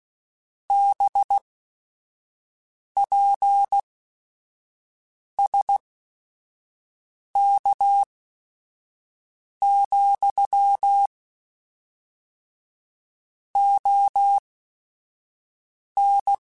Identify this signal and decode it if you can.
Morse code, heard as a keyed tone, about 16 wpm, spaced as for 5 wpm, 800 Hz. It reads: BPSK, ON